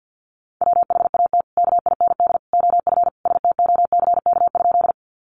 Morse code, heard as a keyed tone, about 50 wpm, 700 Hz. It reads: WHAT KIND OF STOCK?
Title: Sentences from Top 800 Words 50wpm